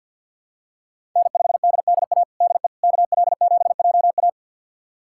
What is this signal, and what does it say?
Morse code, heard as a keyed tone, about 53 wpm, 700 Hz. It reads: N5DDA DE KL7JA
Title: USA Callsign DE Callsign - 5 Characters - 50wpm